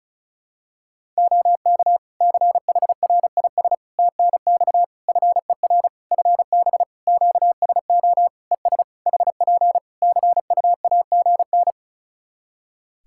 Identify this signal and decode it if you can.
Morse code, heard as a keyed tone, about 36 wpm, 700 Hz. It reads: OK CHRIS TNX FER FB QSO ES HP CUAGN
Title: Practice QSO - Exchange 4 - Ending - 35wpm